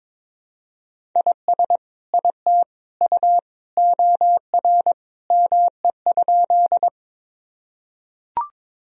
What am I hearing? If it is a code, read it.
Morse code, heard as a keyed tone, about 22 wpm, 700 Hz. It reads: IS IT U OR ME?
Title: Sentences from Top 200 Words and Abbreviations 22wpm